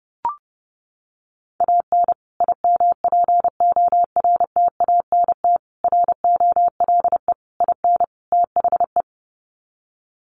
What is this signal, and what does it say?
Morse code, heard as a keyed tone, about 30 wpm, 700 Hz. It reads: AN IMPORTANT ROLE IN THE
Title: Sets of 5 Words Encore2 30wpm